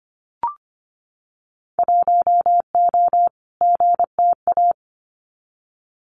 Morse code, heard as a keyed tone, about 24 wpm, 700 Hz. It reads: 1O GTA
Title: Winter Field Day Exchanges 25wpm